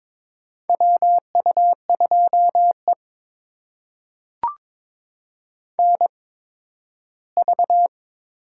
Morse code, heard as a keyed tone, about 23 wpm, spaced as for 13 wpm, 700 Hz. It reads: WU2E NV
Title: USA Call Signs - 4 Characters - 8x Character Spacing - 22wpm